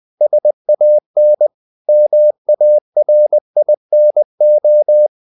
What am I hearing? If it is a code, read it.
Morse code, heard as a keyed tone, about 20 wpm, 600 Hz. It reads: SAN MARINO
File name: CW / Practice / NationalCapitals20